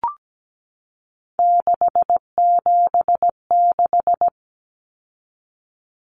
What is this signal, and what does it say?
Morse code, heard as a keyed tone, about 17 wpm, 700 Hz. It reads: 676
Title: Numbers - 3-Digit 17wpm